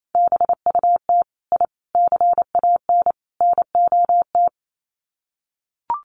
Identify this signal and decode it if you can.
Morse code, heard as a keyed tone, about 28 wpm, 700 Hz. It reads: BUT I CAN NOT
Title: Sets of 4 Words 28wpm